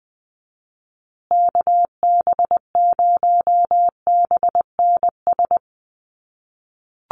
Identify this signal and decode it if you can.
Morse code, heard as a keyed tone, about 19 wpm, 700 Hz. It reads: KB0BNS
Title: USA Call Signs Repeated 2x 20wpm